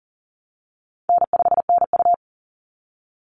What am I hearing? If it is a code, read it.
Morse code, heard as a keyed tone, about 40 wpm, 700 Hz. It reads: N5NU
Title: Easy Call Signs Repeated 2x - 40wpm